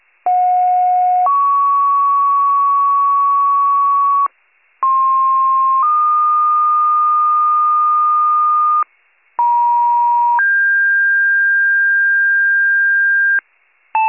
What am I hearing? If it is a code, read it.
Two-tone page, heard as a one-second tone, then a three-second tone, about 730 then 1120 Hz, about 1060 then 1220 Hz, about 950 then 1600 Hz. It is heard from 0.3 to 13.4 s.